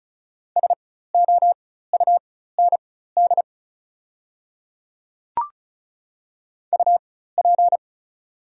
Morse code, heard as a keyed tone, about 34 wpm, spaced as for 16 wpm, 690 Hz. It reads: SOUND UP
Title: 4x Character Spacing - Top 100 Words - 35wpm